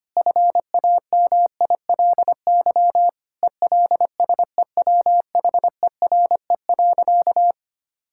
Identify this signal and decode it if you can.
Morse code, heard as a keyed tone, about 25 wpm, 700 Hz. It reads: FAMILY ELSEWHERE.
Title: Sentences from Top 1600 Words 25wpm